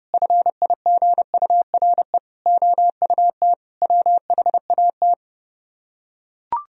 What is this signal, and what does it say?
Morse code, heard as a keyed tone, about 30 wpm, 700 Hz. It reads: FIGURE OUT WHAT